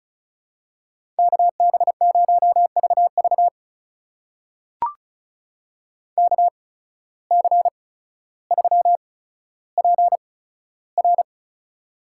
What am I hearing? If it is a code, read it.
Morse code, heard as a keyed tone, about 36 wpm, spaced as for 15 wpm, 700 Hz. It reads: KB0VV KC3PR
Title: USA Call Signs - 5 Characters - 8x Character Spacing - 35wpm